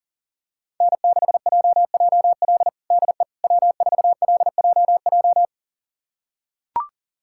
Morse code, heard as a keyed tone, about 40 wpm, 700 Hz. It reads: N6JJL DE W4LJJ
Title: USA Callsign DE Callsign - 5 Characters - 40wpm